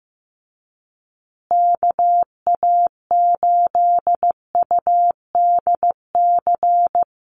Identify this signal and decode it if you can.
Morse code, heard as a keyed tone, about 15 wpm, 700 Hz. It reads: KA8UDC
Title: USA Call Signs Repeated 2x 15wpm